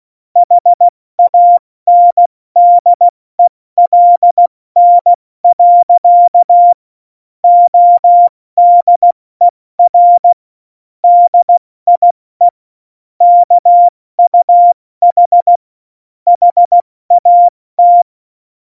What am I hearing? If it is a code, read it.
Morse code, heard as a keyed tone, about 16 wpm, 700 Hz. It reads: HANDELN. ODER DIE KUH HAT
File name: Morsekurs / Witze / Witz 14 Juli 2020, 70 F80